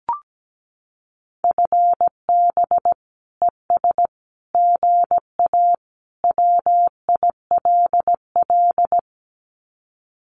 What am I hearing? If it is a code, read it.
Morse code, heard as a keyed tone, about 17 wpm, 700 Hz. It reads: FB ES GA WILL
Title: Common QSO Phrases Encore 17wpm